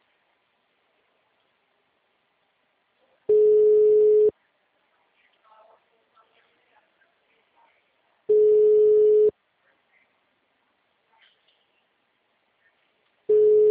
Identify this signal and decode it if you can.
Ringback tone (Europe, 425 Hz).